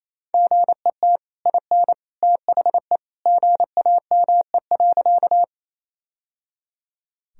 Morse code, heard as a keyed tone, about 28 wpm, 700 Hz. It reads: GET IN THE GAME.
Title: Sentences from Top 1000 Words - Review Part 3 - 28wpm